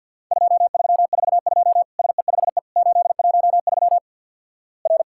Morse code, heard as a keyed tone, about 50 wpm, 700 Hz. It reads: 2342 SE5E 813 R
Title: Worked All Europe DX Contest - QTC Exchange - 50wpm